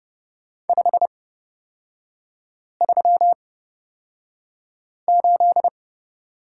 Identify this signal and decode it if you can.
Morse code, heard as a keyed tone, about 30 wpm, spaced as for 5 wpm, 700 Hz. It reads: 538
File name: Lesson 38e - Letter q - Three Characters 30wpm